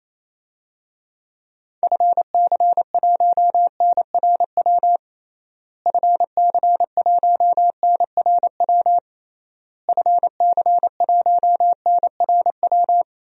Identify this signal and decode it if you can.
Morse code, heard as a keyed tone, about 28 wpm, 700 Hz. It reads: FC1NRW FC1NRW FC1NRW
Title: DXCC-100 Call Signs - 6 Characters - Repeated 3x - 28wpm